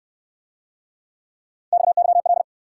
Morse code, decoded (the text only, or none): XYL